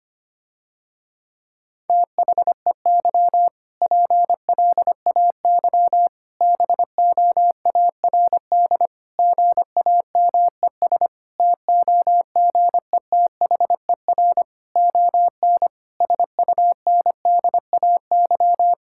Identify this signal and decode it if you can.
Morse code, heard as a keyed tone, about 25 wpm, 700 Hz. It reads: THEY PLAY BOARD GAMES TOGETHER ON SUNDAY
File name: practice / 25wpm / Difficult Sentences with Hints - 25wpm